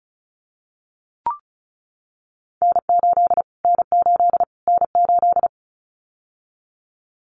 Morse code, heard as a keyed tone, about 34 wpm, 700 Hz. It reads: N8 N8 N8
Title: Callsign Prefixes for Top 25 DXCC - Repeated 3x - 35wpm